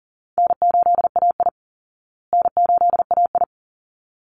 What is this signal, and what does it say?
Morse code, heard as a keyed tone, about 40 wpm, 700 Hz. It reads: N8AI N8AI
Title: Call Signs Repeated 3x - 40wpm